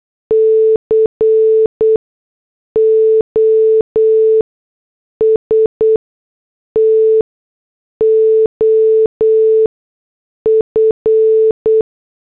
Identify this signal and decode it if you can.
Morse code, heard as a keyed tone, about 8 wpm, spaced as for 6 wpm, 440 Hz. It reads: COST OF